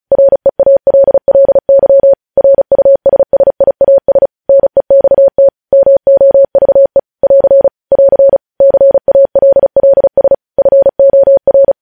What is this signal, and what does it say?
Morse code, heard as a keyed tone, about 36 wpm, 550 Hz. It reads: REALLY RUSSIAS NEXT MOVE <AR> <AR> CALLS FOR
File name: podcast / News headlines-1x-35x35